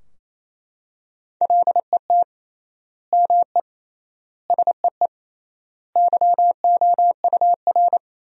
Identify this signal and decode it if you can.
Morse code, heard as a keyed tone, about 28 wpm, 700 Hz. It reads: LET ME SEE YOUR
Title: 3x Word-Spacing - Sentences from Top 500 Words - 28wpm